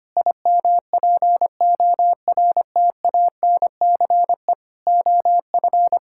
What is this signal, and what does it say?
Morse code, heard as a keyed tone, about 25 wpm, 700 Hz. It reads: IMPORTANCE OF
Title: Sentences from Top 1600 Words 25wpm